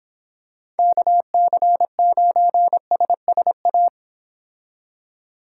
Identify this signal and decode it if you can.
Morse code, heard as a keyed tone, about 26 wpm, 700 Hz. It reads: KC9SSA